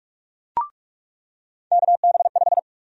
Morse code, heard as a keyed tone, about 46 wpm, 700 Hz. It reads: KB5
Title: USA Call Sign Prefixes - 45wpm